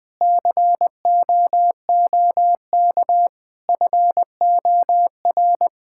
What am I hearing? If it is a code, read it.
Morse code, heard as a keyed tone, about 20 wpm, 700 Hz. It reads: COOK FOR
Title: Sentences from Top 1000 Words - Review Part 5 - 20wpm